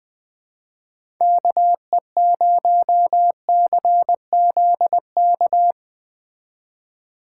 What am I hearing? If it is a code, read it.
Morse code, heard as a keyed tone, about 20 wpm, 700 Hz. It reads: KE0CZK
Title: Speed Racing - USA 2x3 Call Signs 17wpm